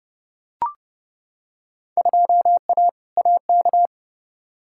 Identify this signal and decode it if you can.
Morse code, heard as a keyed tone, about 32 wpm, 700 Hz. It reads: 2A AK
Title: Field Day Exchanges Encore - 30wpm